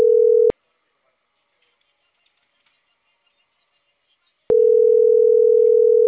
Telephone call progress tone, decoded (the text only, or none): ringback tone